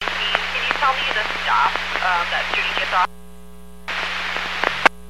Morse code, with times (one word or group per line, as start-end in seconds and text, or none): none